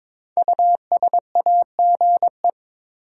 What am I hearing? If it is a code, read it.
Morse code, heard as a keyed tone, about 22 wpm, 700 Hz. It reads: USAGE
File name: 5 Letter Words - First 3 Letters Given - 22wpm